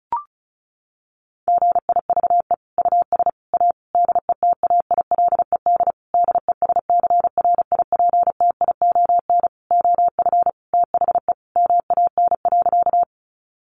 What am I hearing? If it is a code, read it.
Morse code, heard as a keyed tone, about 34 wpm, 700 Hz. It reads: GIVE US A DETAILED DESCRIPTION OF THE MAN.
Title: Sentences from Top 1500 Words 35wpm